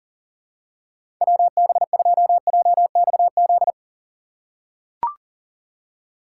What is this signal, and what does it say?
Morse code, heard as a keyed tone, about 40 wpm, 700 Hz. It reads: WB2JXZ
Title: USA Call Signs - 6 Characters - 3x Character Spacing - 40wpm